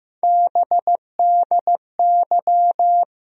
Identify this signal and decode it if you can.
Morse code, heard as a keyed tone, about 15 wpm, 700 Hz. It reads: BDY
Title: USA Call Sign Postfixes - 3 Letters 15wpm